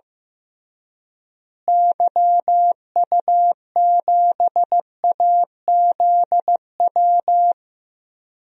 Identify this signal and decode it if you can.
Morse code, heard as a keyed tone, about 15 wpm, 700 Hz. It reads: YU7AZW
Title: DXCC-100 Call Signs - 6 Characters - Repeated 3x - 15wpm